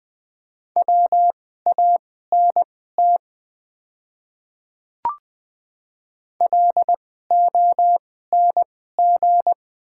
Morse code, heard as a keyed tone, about 20 wpm, spaced as for 14 wpm, 700 Hz. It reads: WANT LONG
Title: 2x Character Spacing - Top 200 Words - 20wpm